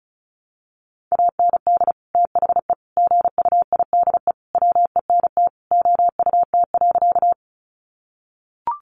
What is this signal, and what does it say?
Morse code, heard as a keyed tone, about 34 wpm, 700 Hz. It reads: AND THE GUIDE WENT OUT.
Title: Sentences from Top 600 Words 35wpm